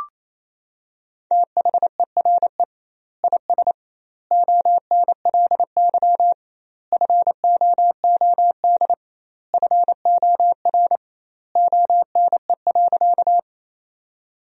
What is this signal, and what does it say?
Morse code, heard as a keyed tone, about 28 wpm, 700 Hz. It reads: THERE IS ONLY FOOD FOR ONE.
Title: 2x Word-Spacing - Sentences from Top 300 Words - 28wpm